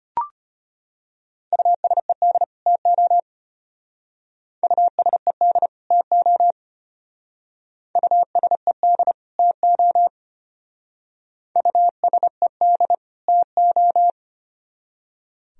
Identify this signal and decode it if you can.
Morse code, heard as a keyed tone, about 31 wpm, 700 Hz. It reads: USED TO USED TO USED TO USED TO
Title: Speed-Racing - Sets of 2 Words 25wpm